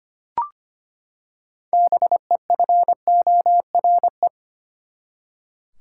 Morse code, heard as a keyed tone, about 25 wpm, 700 Hz. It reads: BEFORE
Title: Six Letter Words - 25wpm